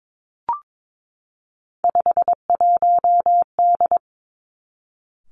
Morse code, heard as a keyed tone, about 22 wpm, 700 Hz. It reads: 51D
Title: Three Challenging Letter-Number Combinations 22wpm